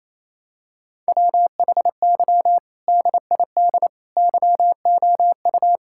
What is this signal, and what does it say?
Morse code, heard as a keyed tone, about 28 wpm, 700 Hz. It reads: WHY DID YOU
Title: Sentences from Top 1600 Words 28wpm